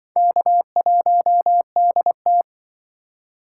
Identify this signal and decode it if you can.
Morse code, heard as a keyed tone, about 25 wpm, 700 Hz. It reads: K1DT
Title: Speed Racing - USA 1x2 Call Signs 20wpm